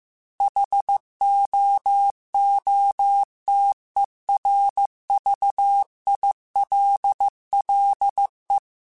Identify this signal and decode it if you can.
Morse code, heard as a keyed tone, about 15 wpm, 800 Hz. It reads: HOOTERVILLE